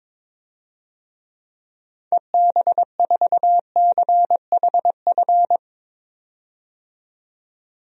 Morse code, heard as a keyed tone, about 22 wpm, 700 Hz. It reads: EB4CHF
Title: DXCC-100 Call Signs - 6 Characters - 22wpm